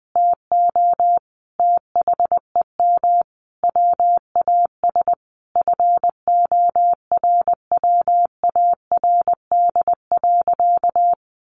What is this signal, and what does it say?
Morse code, heard as a keyed tone, about 20 wpm, 700 Hz. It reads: TO THEM WAS FORWARD.
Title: Sentences from Top 1500 Words 20wpm